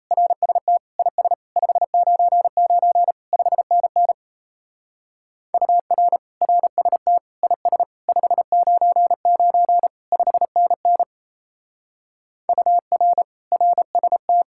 Morse code, heard as a keyed tone, about 34 wpm, 700 Hz. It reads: RST IS 599 5NN UR RST IS 599 5NN UR RST
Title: Speed-Racing Common QSO Phrases - 28wpm